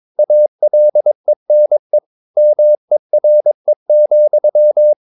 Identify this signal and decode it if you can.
Morse code, heard as a keyed tone, about 22 wpm, 600 Hz. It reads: ALENE MERE,
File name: Ferie QRS 22wpm